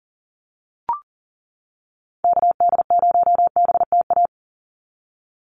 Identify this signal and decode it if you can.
Morse code, heard as a keyed tone, about 40 wpm, 700 Hz. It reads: KD0BTA
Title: USA Call Signs 40wpm